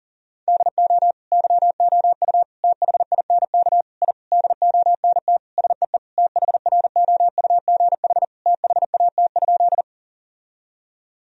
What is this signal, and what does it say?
Morse code, heard as a keyed tone, about 40 wpm, 700 Hz. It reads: DO YOU THINK I DONT SEE THROUGH THAT?
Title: Sentences from Top 1000 Words - Review Part 1 - 40wpm